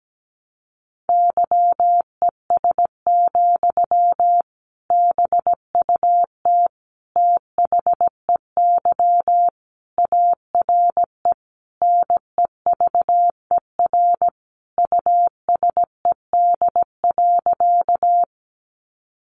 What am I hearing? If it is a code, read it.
Morse code, heard as a keyed tone, about 17 wpm, 700 Hz. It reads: YES, BUT THEY ARE NEVER USED.
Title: Sentences from Top 600 Words 17wpm